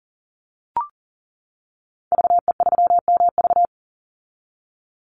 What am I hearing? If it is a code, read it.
Morse code, heard as a keyed tone, about 40 wpm, 700 Hz. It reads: VE3MV
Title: Call Signs - 40wpm